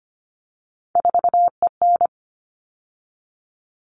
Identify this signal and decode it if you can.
Morse code, heard as a keyed tone, about 25 wpm, 700 Hz. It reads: 4EN
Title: All Cut Numbers - 3-Digit 25wpm